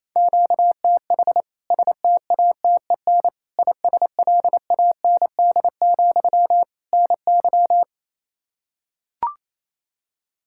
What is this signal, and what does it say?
Morse code, heard as a keyed tone, about 27 wpm, 700 Hz. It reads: QTH STATEN ISLAND, NY
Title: K1USN SST QSO Practice 28wpm